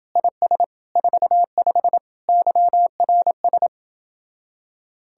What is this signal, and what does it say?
Morse code, heard as a keyed tone, about 29 wpm, 700 Hz. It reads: IS 45 YRS